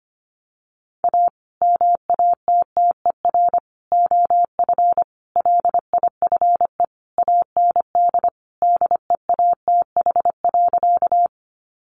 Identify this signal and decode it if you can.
Morse code, heard as a keyed tone, about 25 wpm, 700 Hz. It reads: A MATTER OF LIFE AND DEATH.
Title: Sentences from Top 1100 Words 25wpm